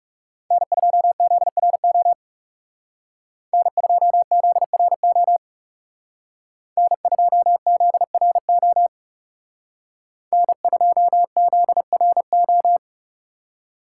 Morse code, heard as a keyed tone, about 36 wpm, 700 Hz. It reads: N2ZRO N2ZRO N2ZRO N2ZRO
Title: Speed Racing - USA 1x3 Call Signs 30wpm